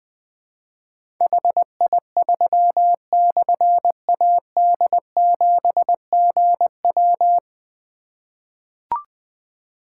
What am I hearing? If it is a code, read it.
Morse code, heard as a keyed tone, about 20 wpm, 700 Hz. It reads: HI3/AD7GW
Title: Difficult Call Signs 20wpm